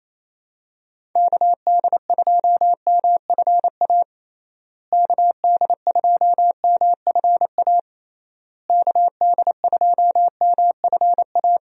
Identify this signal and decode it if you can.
Morse code, heard as a keyed tone, about 28 wpm, 700 Hz. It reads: KD2MFA KD2MFA KD2MFA
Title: USA Call Signs - 6 Characters - Repeated 3x - 28wpm